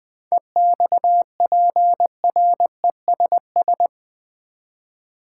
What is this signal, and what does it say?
Morse code, heard as a keyed tone, about 21 wpm, 700 Hz. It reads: EXPRESS
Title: Sentences from Top 1100 Words - New Word Drill Alt v2 - 20wpm